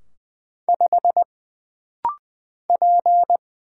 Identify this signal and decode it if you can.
Morse code, heard as a keyed tone, about 20 wpm, 700 Hz. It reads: 5 P
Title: Single Letter-Number - Rapid-Fire 20wpm